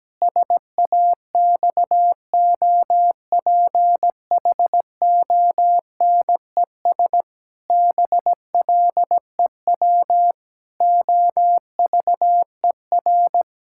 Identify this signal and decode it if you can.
Morse code, heard as a keyed tone, about 17 wpm, 700 Hz. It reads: SAXOPHONES BLEW OVER
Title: Pangram Challenge v2 - 17wpm